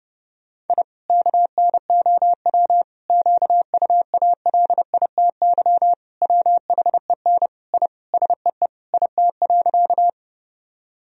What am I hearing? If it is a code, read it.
Morse code, heard as a keyed tone, about 30 wpm, 700 Hz. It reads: I KNOW QUALITY WHEN I SEE IT.
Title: Sentences from Top 1000 Words - Review Part 4 - 30wpm